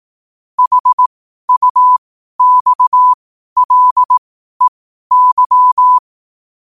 Morse code, heard as a keyed tone, about 18 wpm, spaced as for 12 wpm, 1000 Hz. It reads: HUXLEY
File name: qotdfiles / 12wpm